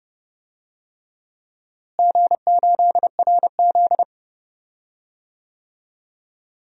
Morse code, heard as a keyed tone, about 31 wpm, 700 Hz. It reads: G8RZ
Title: DXCC-100 Call Signs - 4 Characters - Repeated 3x - 30wpm